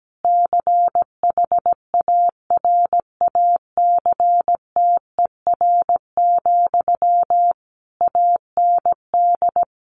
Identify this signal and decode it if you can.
Morse code, heard as a keyed tone, about 17 wpm, 700 Hz. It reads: CHARACTER, AND
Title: Sentences from Top 1400 Words 17wpm